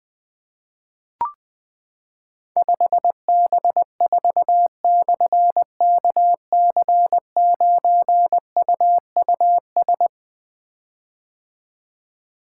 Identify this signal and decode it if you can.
Morse code, heard as a keyed tone, about 20 wpm, 700 Hz. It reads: 5B4/KC9UUS
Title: Difficult Call Signs 20wpm